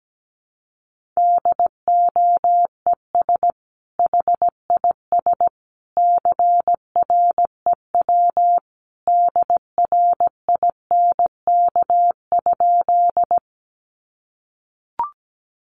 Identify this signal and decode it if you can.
Morse code, heard as a keyed tone, about 17 wpm, 700 Hz. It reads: DOES HIS CREW DRINK?
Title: Sentences from Top 1500 Words 17wpm